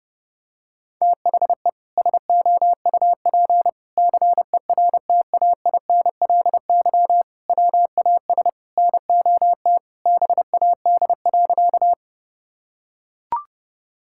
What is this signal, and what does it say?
Morse code, heard as a keyed tone, about 30 wpm, 700 Hz. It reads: THE SOUP CERTAINLY WAS NOT BAD.
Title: Sentences from Top 1000 Words - Review Part 6 - 30wpm